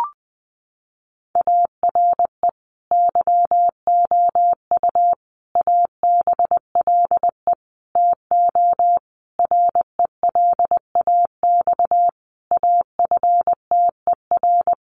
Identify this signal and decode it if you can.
Morse code, heard as a keyed tone, about 20 wpm, 700 Hz. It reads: ARE YOU ABLE TO RELAX AFTER
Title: Sentences from Top 1400 Words 20wpm